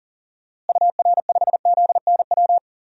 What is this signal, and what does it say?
Morse code, heard as a keyed tone, about 40 wpm, 700 Hz. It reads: UR5ZNW
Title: DXCC-100 Call Signs - 6 Characters - Repeated 3x - 40wpm